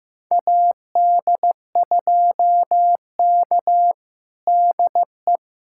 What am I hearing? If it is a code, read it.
Morse code, heard as a keyed tone, about 16 wpm, 700 Hz. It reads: AD2K DE
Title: USA Callsign DE Callsign 15wpm